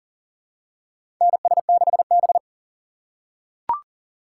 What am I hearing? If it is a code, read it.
Morse code, heard as a keyed tone, about 40 wpm, 700 Hz. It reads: NS6B
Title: USA Call Signs - 3x Character Spacing - 40wpm